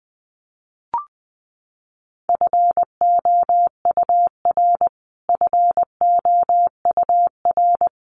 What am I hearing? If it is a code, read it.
Morse code, heard as a keyed tone, about 20 wpm, 700 Hz. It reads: FOUR FOUR
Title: Top 300 Words Repeated 20wpm